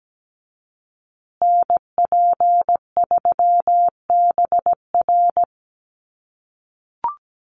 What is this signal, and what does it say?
Morse code, heard as a keyed tone, about 17 wpm, 700 Hz. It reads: NP3BR
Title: USA Call Signs Repeated 2x 17wpm